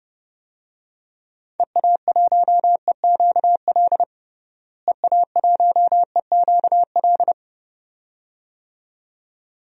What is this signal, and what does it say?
Morse code, heard as a keyed tone, about 29 wpm, 700 Hz. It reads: EA1EQL EA1EQL
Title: DXCC-100 Call Signs - 6 Characters - Repeated 2x - 30wpm